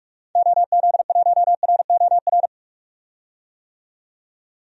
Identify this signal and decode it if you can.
Morse code, heard as a keyed tone, about 45 wpm, 700 Hz. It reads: OZ1ROR
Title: DXCC-100 Call Signs - 6 Characters - 45wpm